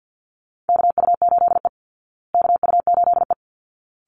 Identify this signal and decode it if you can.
Morse code, heard as a keyed tone, about 50 wpm, 700 Hz. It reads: KU8E KU8E
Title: Call Signs Repeated 3x - 50wpm